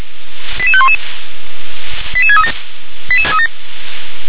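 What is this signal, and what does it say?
Five-tone selective call (ZVEI), heard as quick runs of five tones, tones of about 70 ms. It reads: 88511 88637
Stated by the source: ZVEI five-tone call received on SDR.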